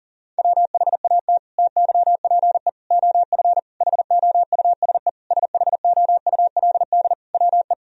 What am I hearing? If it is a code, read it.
Morse code, heard as a keyed tone, about 40 wpm, 700 Hz. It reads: WHAT TYPE OF HOUSE SHOULD WE
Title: Sentences from Top 1000 Words - Review Part 3 - 40wpm